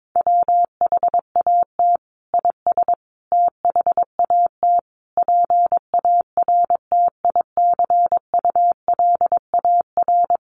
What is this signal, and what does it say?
Morse code, heard as a keyed tone, about 22 wpm, 700 Hz. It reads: WHAT IS THAT PARTICULAR
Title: Sentences from Top 1500 Words 22wpm